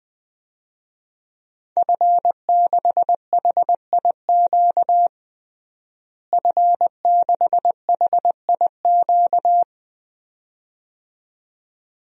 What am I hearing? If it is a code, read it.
Morse code, heard as a keyed tone, about 20 wpm, 700 Hz. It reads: F6HIQ F6HIQ
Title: DXCC-100 Call Signs - 5 Characters - Repeated 2x - 20wpm